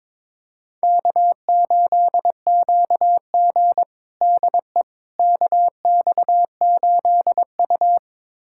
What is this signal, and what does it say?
Morse code, heard as a keyed tone, about 22 wpm, 700 Hz. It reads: K8QG DE KX8U
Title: USA Callsign DE Callsign - 4 Characters - 22wpm